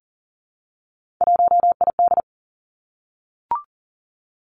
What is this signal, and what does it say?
Morse code, heard as a keyed tone, about 38 wpm, 700 Hz. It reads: 1ID
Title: Three Challenging Letter-Number Combinations 40wpm